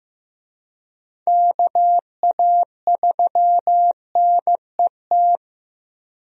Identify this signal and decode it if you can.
Morse code, heard as a keyed tone, about 15 wpm, 700 Hz. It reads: KA3NET
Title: USA Call Signs - 6 Characters - 15wpm